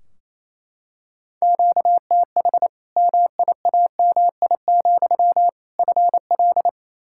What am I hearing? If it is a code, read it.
Morse code, heard as a keyed tone, about 28 wpm, 700 Hz. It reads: QTH MIAMI, FL
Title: Practice QSO - Exchange 1 - Beginning - 28wpm